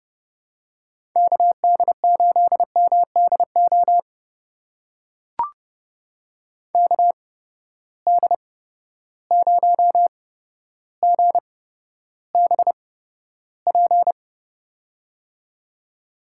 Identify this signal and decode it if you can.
Morse code, heard as a keyed tone, about 30 wpm, spaced as for 13 wpm, 700 Hz. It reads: KD8MDO KD0GBP
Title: USA Call Signs - 6 Characters - 8x Character Spacing - 30wpm